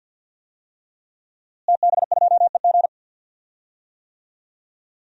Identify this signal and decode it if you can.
Morse code, heard as a keyed tone, about 50 wpm, 700 Hz. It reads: TB2EG
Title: DXCC-100 Call Signs - 5 Characters - Repeated 3x - 50wpm